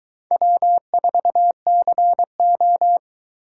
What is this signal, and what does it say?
Morse code, heard as a keyed tone, about 22 wpm, 700 Hz. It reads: W4CO